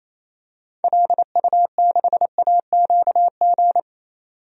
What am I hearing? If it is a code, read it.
Morse code, heard as a keyed tone, about 28 wpm, 700 Hz. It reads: LU6AQG
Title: DXCC-100 Call Signs - 6 Characters - 28wpm